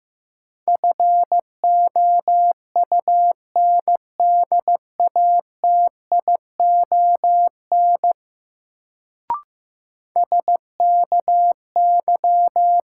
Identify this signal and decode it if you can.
Morse code, heard as a keyed tone, about 15 wpm, 700 Hz. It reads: FOUNDATION SKY
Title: Sentences from Top 1200 Words - New Word Drill - 15wpm